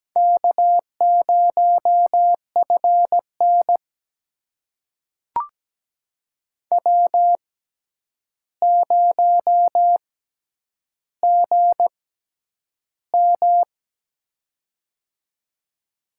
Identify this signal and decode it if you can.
Morse code, heard as a keyed tone, about 17 wpm, spaced as for 9 wpm, 700 Hz. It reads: K0FN W0GM
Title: USA Call Signs - 4 Characters - 6x Character Spacing - 17wpm